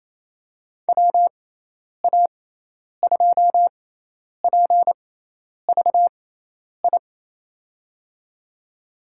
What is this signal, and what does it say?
Morse code, heard as a keyed tone, about 28 wpm, spaced as for 10 wpm, 700 Hz. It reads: WA2PVI
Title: USA Call Signs - 6x Character Spacing - 28wpm